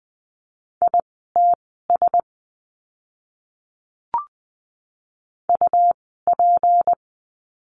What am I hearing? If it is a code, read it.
Morse code, heard as a keyed tone, about 20 wpm, spaced as for 15 wpm, 700 Hz. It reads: ITS UP
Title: 206 Lesson 33f - Letter j - Words 20wpm